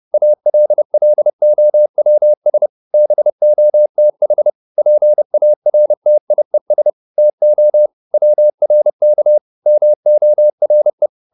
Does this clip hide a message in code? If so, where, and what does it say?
Yes: Morse code from 0.1 to 11.1 s, heard as a keyed tone, about 30 wpm, 600 Hz. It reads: ALLOWS BOTH PARTIES TO WRK MORE